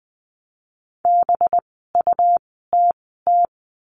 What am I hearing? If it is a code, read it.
Morse code, heard as a keyed tone, about 20 wpm, spaced as for 14 wpm, 700 Hz. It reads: BUTT
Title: 185 Lesson 30f - Letter b - Words 20wpm